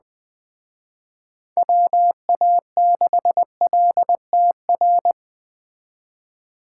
Morse code, heard as a keyed tone, about 20 wpm, 700 Hz. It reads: WA6LTR
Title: Speed Racing - USA Call Signs 17wpm